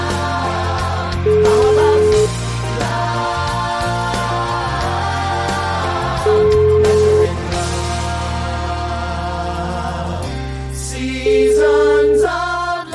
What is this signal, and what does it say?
Ringback tone (Europe, 425 Hz), with noise.